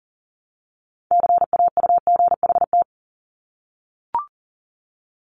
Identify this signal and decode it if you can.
Morse code, heard as a keyed tone, about 37 wpm, 700 Hz. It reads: CAUGHT
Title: Multi-letter Phonemes Repeated 3x 40wpm